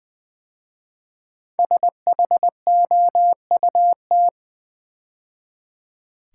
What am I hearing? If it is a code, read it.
Morse code, heard as a keyed tone, about 20 wpm, 700 Hz. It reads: SHOUT
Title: 5 Letter Words - First 3 Letters Given - 20wpm